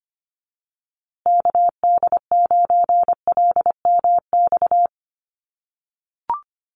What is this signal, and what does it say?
Morse code, heard as a keyed tone, about 25 wpm, 700 Hz. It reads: KD9LMX
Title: USA Call Signs Repeated 3x 25wpm